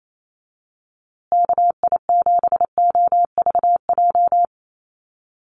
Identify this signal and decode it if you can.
Morse code, heard as a keyed tone, about 28 wpm, 700 Hz. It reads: KI7OVJ
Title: USA Call Signs Repeated 3x 28wpm